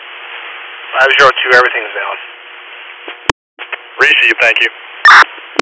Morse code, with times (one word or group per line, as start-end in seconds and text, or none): none